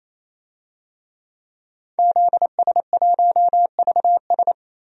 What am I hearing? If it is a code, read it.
Morse code, heard as a keyed tone, about 28 wpm, 700 Hz. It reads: ZS1VS
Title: DXCC-100 Call Signs - 5 Characters - 28wpm